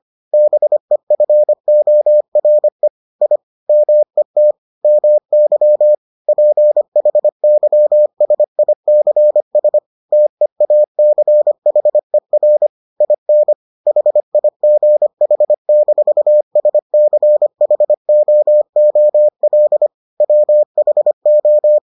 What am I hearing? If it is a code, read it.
Morse code, heard as a keyed tone, about 25 wpm, 600 Hz. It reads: BEFORE I MET MY PHYSICS TEACHER IN HIGH-SCHOOL WHO